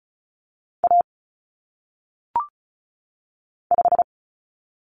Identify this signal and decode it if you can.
Morse code, heard as a keyed tone, about 36 wpm, 700 Hz. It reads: A 5
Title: Common Cut Numbers - 1-Digit 35wpm